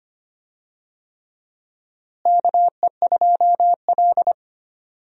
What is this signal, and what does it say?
Morse code, heard as a keyed tone, about 25 wpm, 700 Hz. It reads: KE2L
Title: USA Call Signs - 4 Characters - Repeated 3x - 25wpm